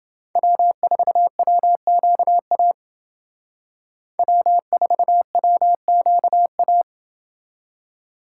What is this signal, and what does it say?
Morse code, heard as a keyed tone, about 28 wpm, 700 Hz. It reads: W4WQA W4WQA